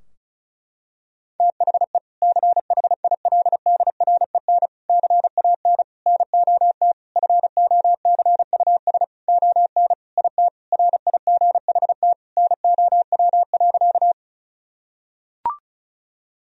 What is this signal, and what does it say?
Morse code, heard as a keyed tone, about 36 wpm, 700 Hz. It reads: THE CHILDREN CAN NOT FOCUS ON IT RIGHT NOW.
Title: Sentences from Top 1000 Words - Review Part 4 - 35wpm